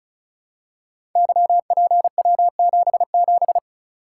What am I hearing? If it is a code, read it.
Morse code, heard as a keyed tone, about 36 wpm, 700 Hz. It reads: YPW77
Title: Code Groups - 5 Letter:Numbers - 35wpm